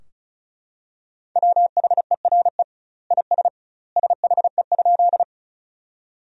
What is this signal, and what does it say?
Morse code, heard as a keyed tone, about 36 wpm, 700 Hz. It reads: WHERE IS SHE?
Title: 2x Word-Spacing - Sentences from Top 200 Words - 35wpm